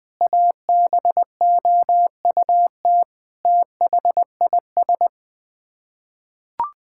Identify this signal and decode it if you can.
Morse code, heard as a keyed tone, about 19 wpm, 700 Hz. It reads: ABOUT THIS
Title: Sets of 9 Words - 20wpm